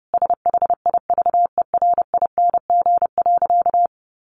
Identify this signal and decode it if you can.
Morse code, heard as a keyed tone, about 30 wpm, 700 Hz. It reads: SHIVERING.
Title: A Sound Of Thunder 30wpm